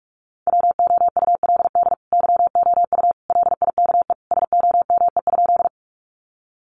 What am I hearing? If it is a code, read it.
Morse code, heard as a keyed tone, about 44 wpm, 700 Hz. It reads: WOULD YOU LIKE SOME?
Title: Sentences from Top 100 Words 45wpm